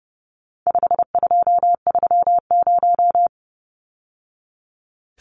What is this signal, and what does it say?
Morse code, heard as a keyed tone, about 30 wpm, 700 Hz. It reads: 5230
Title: Numbers - 4-Digit 30wpm